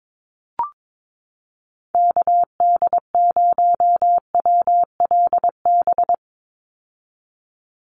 Morse code, heard as a keyed tone, about 22 wpm, 700 Hz. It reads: KD0WLB
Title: USA Call Signs 22wpm